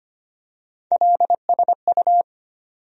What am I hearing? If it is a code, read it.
Morse code, heard as a keyed tone, about 25 wpm, 700 Hz. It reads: LSU
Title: USA Call Sign Postfixes - 3 Letters 25wpm